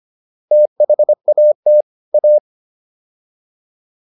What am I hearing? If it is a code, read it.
Morse code, heard as a keyed tone, about 25 wpm, 600 Hz. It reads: THAT A